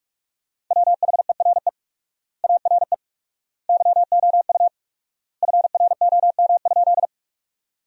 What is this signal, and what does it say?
Morse code, heard as a keyed tone, about 46 wpm, 700 Hz. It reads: WHERE ARE YOU FROM?
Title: 4x Word-Spacing - Sentences from Top 200 Words - 45wpm